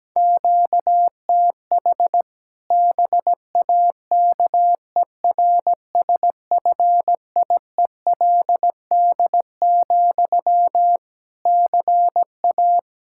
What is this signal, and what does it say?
Morse code, heard as a keyed tone, about 17 wpm, 700 Hz. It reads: QTH BAKERSFIELD, CA
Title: Practice QSO - Exchange 1 - Beginning - 17wpm